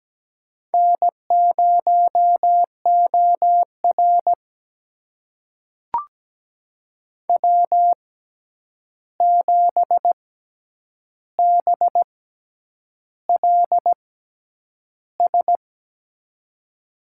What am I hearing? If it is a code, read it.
Morse code, heard as a keyed tone, about 17 wpm, spaced as for 8 wpm, 700 Hz. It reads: N0OR W7BLS
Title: USA Call Signs - 6x Character Spacing - 17wpm